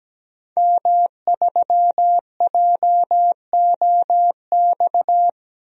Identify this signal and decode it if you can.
Morse code, heard as a keyed tone, about 17 wpm, 700 Hz. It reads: M3JOX
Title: DXCC-100 Call Signs - 5 Characters - 17wpm